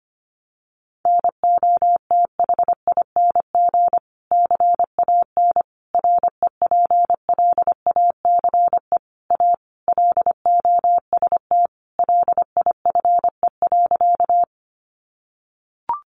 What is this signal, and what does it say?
Morse code, heard as a keyed tone, about 25 wpm, 700 Hz. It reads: NOTHING CAN REPLACE A LOST LIFE.
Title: Sentences from Top 1400 Words 25wpm